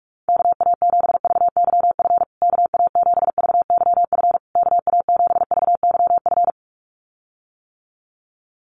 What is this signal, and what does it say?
Morse code, heard as a keyed tone, about 44 wpm, 700 Hz. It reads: KA7VYF KA7VYF KA7VYF
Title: USA Call Signs Repeated 3x 45wpm